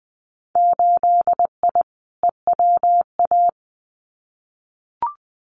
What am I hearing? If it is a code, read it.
Morse code, heard as a keyed tone, about 20 wpm, 700 Hz. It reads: 8I EWA
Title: Winter Field Day Exchanges Encore 20wpm